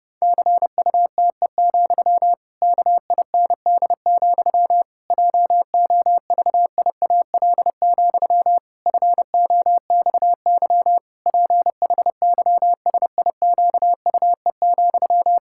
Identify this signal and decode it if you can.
Morse code, heard as a keyed tone, about 30 wpm, 700 Hz. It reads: CUTE, KIND, JOVIAL, FOXY PHYSIQUE,
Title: Pangram Challenge v2 - 30wpm